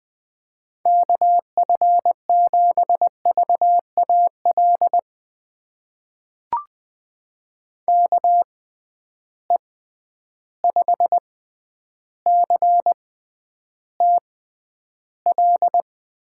Morse code, heard as a keyed tone, about 20 wpm, spaced as for 10 wpm, 700 Hz. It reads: KF7VAL KE5CTL
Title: USA Call Signs - 6 Characters - 6x Character Spacing - 20wpm